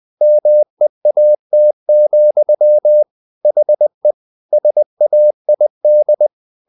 Morse code, heard as a keyed tone, about 20 wpm, 600 Hz. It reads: MEAT, HE SAID
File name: the old man and the sea part 2 20wpm